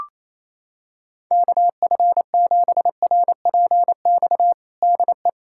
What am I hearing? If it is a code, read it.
Morse code, heard as a keyed tone, about 28 wpm, 700 Hz. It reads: KF7RPX DE
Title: USA Callsign DE Callsign - 6 Characters - 28wpm